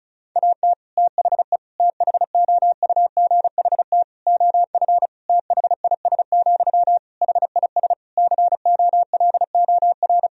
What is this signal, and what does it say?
Morse code, heard as a keyed tone, about 34 wpm, 700 Hz. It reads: AT THE THOUGHT OF THIS, HIS COLOR
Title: Sentences from Top 1000 Words - Review Part 2 - 35wpm